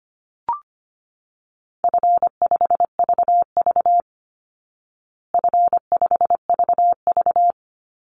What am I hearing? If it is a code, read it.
Morse code, heard as a keyed tone, about 25 wpm, 700 Hz. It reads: F5VV F5VV
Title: Easy Call Signs Repeated 2x - 25wpm